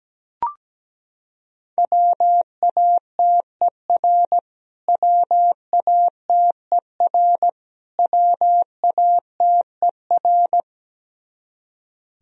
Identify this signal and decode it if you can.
Morse code, heard as a keyed tone, about 17 wpm, 700 Hz. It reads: WATER WATER WATER
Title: Top 100 Words Repeated 17wpm